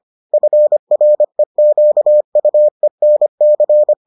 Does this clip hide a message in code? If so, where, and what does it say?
Yes: Morse code from 0.3 to 3.9 s, heard as a keyed tone, about 25 wpm, 600 Hz. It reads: FREQUENC